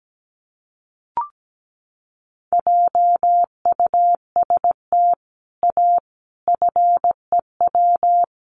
Morse code, heard as a keyed tone, about 17 wpm, 700 Hz. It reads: JUST A FEW
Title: Sets of 5 Words Encore 17wpm